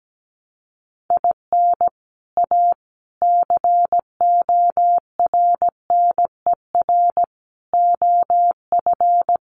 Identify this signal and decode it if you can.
Morse code, heard as a keyed tone, about 17 wpm, 700 Hz. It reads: IN A CORNER OF
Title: Sets of 5 Words Encore2 17wpm